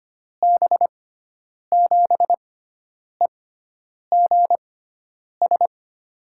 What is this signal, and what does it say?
Morse code, heard as a keyed tone, about 25 wpm, spaced as for 9 wpm, 700 Hz. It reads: B7EGS
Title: USA Call Signs - 6 Characters - 6x Character Spacing - 25wpm